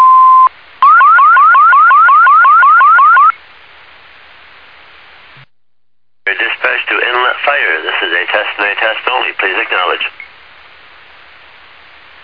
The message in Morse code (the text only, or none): none